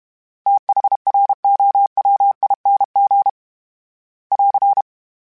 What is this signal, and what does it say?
Morse code, heard as a keyed tone, about 32 wpm, 800 Hz. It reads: THROWING <AR>
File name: TH / TH mp3s / TH Prefixes 3x2 @30